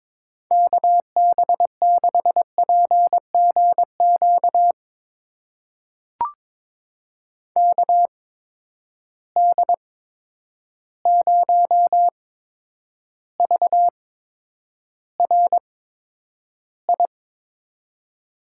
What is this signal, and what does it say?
Morse code, heard as a keyed tone, about 22 wpm, spaced as for 9 wpm, 700 Hz. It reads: KB6PGQ KD0VRI
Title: USA Call Signs - 6 Characters - 8x Character Spacing - 22wpm